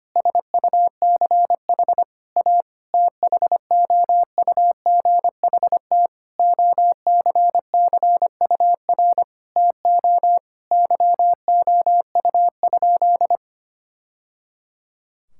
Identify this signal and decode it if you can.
Morse code, heard as a keyed tone, about 25 wpm, 700 Hz. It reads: SUCH A THOUGHT OCCUR TO YOU?
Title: Sentences from Top 1000 Words - Review Part 6 - 25wpm